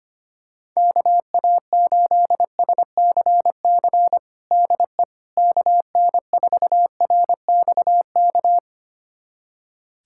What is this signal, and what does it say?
Morse code, heard as a keyed tone, about 25 wpm, 700 Hz. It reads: KA8SCC DE KN4RXK
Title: USA Callsign DE Callsign 25wpm